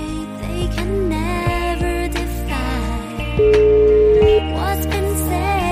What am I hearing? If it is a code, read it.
Ringback tone (Europe, 425 Hz), with noise.